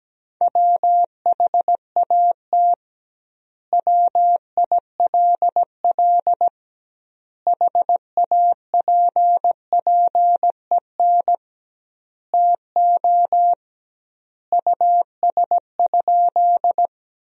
Morse code, heard as a keyed tone, about 17 wpm, 700 Hz. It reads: WHAT WILL HAPPEN TO US?
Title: 2x Word-Spacing - Sentences from Top 400 Words - 17wpm